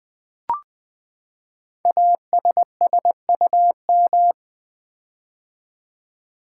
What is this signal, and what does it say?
Morse code, heard as a keyed tone, about 20 wpm, 700 Hz. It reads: ASSUM
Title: Build A Word - 5 Letters At A Time - 10-Letter Words - 20wpm